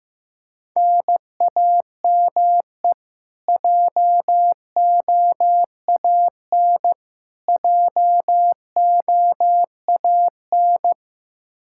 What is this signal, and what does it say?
Morse code, heard as a keyed tone, about 15 wpm, 700 Hz. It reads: NAME JOAN JOAN
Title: Practice QSO - Exchange 1 - Beginning - 15wpm